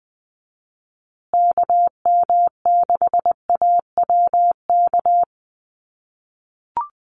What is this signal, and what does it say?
Morse code, heard as a keyed tone, about 20 wpm, 700 Hz. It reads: KM6AWK
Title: USA Call Signs Repeated 2x 20wpm